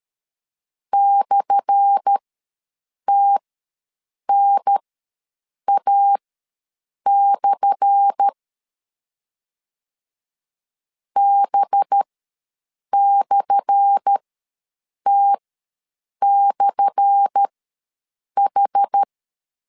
Morse code, heard as a keyed tone, about 13 wpm, spaced as for 7 wpm, 800 Hz. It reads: /TNA/ B/T/H
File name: LEZIONI / 10 SIGNS / 10A SLASH ws 12wpm 3443